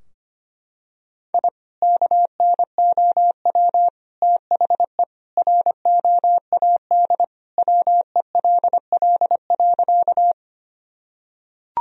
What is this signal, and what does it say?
Morse code, heard as a keyed tone, about 25 wpm, 700 Hz. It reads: I KNOW THE ROAD WELL.
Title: Sentences from Top 1000 Words - Review Part 2 - 25wpm